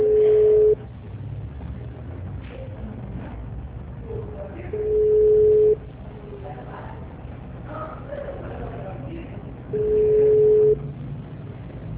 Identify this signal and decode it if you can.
Ringback tone (Europe, 425 Hz), with speech.